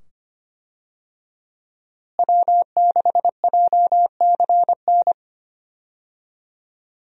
Morse code, heard as a keyed tone, about 25 wpm, 690 Hz. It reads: W6JCN
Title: USA Call Signs - 5 Characters - Repeated 2x - 25wpm